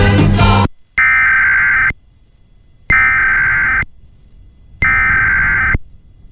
Emergency Alert System (SAME) tones (header bursts), heard as harsh buzzing bursts. It is heard from 1.0 to 5.7 s.